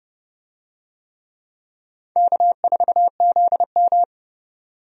Morse code, heard as a keyed tone, about 29 wpm, 700 Hz. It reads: K4ZM
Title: USA Call Signs - 4 Characters - 30wpm